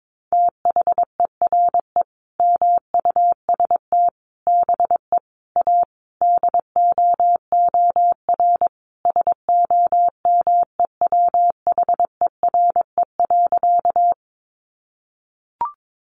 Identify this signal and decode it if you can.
Morse code, heard as a keyed tone, about 22 wpm, 700 Hz. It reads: THERE MUST BE A DOOR SOMEWHERE.
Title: Sentences from Top 1400 Words 22wpm